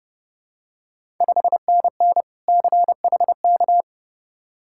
Morse code, heard as a keyed tone, about 30 wpm, 700 Hz. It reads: 5NN CHK
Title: Ontario QSO Party Exchange 30wpm